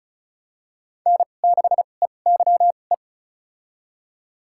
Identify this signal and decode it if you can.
Morse code, heard as a keyed tone, about 36 wpm, spaced as for 25 wpm, 700 Hz. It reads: N6EYE
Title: USA Call Signs - 5 Characters - 2x Character Spacing - 35wpm